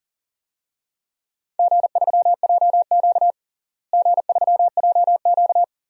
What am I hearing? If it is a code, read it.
Morse code, heard as a keyed tone, about 40 wpm, 700 Hz. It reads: G3JQ G3JQ
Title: DXCC-100 Call Signs - 4 Characters - Repeated 3x - 40wpm